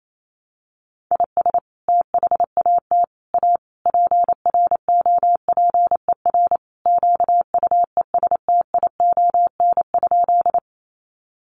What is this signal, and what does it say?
Morse code, heard as a keyed tone, about 28 wpm, 700 Hz. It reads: IS THAT A PROPER QUESTION?
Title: Sentences from Top 1100 Words 28wpm